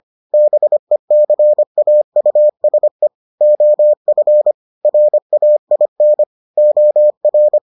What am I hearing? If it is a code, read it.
Morse code, heard as a keyed tone, about 25 wpm, 600 Hz. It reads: BECAUSE OF RAIN OR